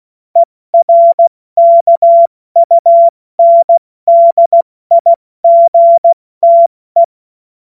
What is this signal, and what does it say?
Morse code, heard as a keyed tone, about 16 wpm, 680 Hz. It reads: ERKUNDIGTE
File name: Morsekurs / Witze / Witz 09. Dezember 2025 70 F80